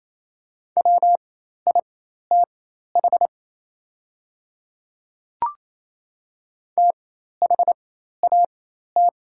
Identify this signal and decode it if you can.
Morse code, heard as a keyed tone, about 27 wpm, spaced as for 13 wpm, 690 Hz. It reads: WITH THAT
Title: 4x Character Spacing - Top 100 Words - 28wpm